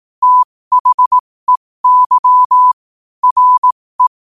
Morse code, heard as a keyed tone, about 18 wpm, spaced as for 15 wpm, 1000 Hz. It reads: THEY RE
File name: qotdfiles / 15wpm